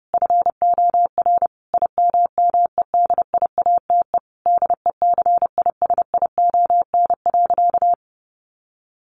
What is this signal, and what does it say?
Morse code, heard as a keyed tone, about 30 wpm, 700 Hz. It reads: FOR IMMEDIATE DECISION.
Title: Sentences from Top 1200 Words 30wpm